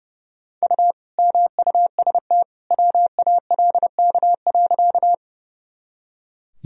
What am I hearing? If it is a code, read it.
Morse code, heard as a keyed tone, about 31 wpm, 700 Hz. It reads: U MUST WALK.
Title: Sentences from Top 300 Words and Abbreviations 30wpm